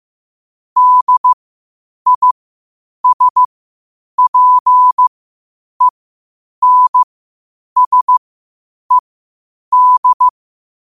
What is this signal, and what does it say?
Morse code, heard as a keyed tone, about 15 wpm, spaced as for 8 wpm, 1000 Hz. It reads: DISPENSED
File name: qotdfiles / 8wpm